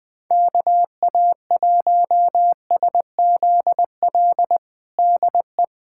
Morse code, heard as a keyed tone, about 21 wpm, 700 Hz. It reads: KA1SZL DE
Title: USA Callsign DE Callsign - 6 Characters - 20wpm